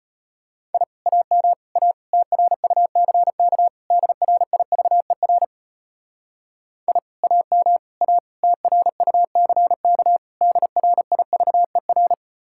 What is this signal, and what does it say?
Morse code, heard as a keyed tone, about 31 wpm, 700 Hz. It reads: I AM A TRUCK DRIVER I AM A TRUCK DRIVER
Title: Speed-Racing Common QSO Phrases - 25wpm